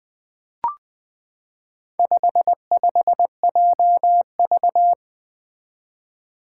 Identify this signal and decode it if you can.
Morse code, heard as a keyed tone, about 20 wpm, 700 Hz. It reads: 55JV
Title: Four Challenging Letter-Number Combinations 20wpm